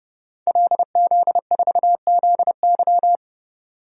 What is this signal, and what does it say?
Morse code, heard as a keyed tone, about 30 wpm, 700 Hz. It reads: LZ4ZY
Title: DXCC-100 Call Signs - 5 Characters - 30wpm